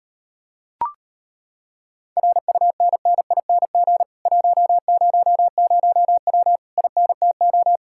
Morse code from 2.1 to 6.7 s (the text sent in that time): RUNNING 100W